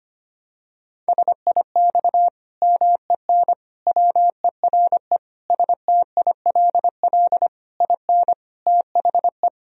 Morse code, heard as a keyed tone, about 25 wpm, 700 Hz. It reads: SIX MEN WERE STILL IN THE
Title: Sentences from Top 1000 Words - Review Part 2 - 25wpm